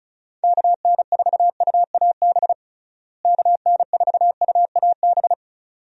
Morse code, heard as a keyed tone, about 36 wpm, 700 Hz. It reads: KN4UAB KN4UAB
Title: USA Call Signs - 6 Characters - Repeated 2x - 35wpm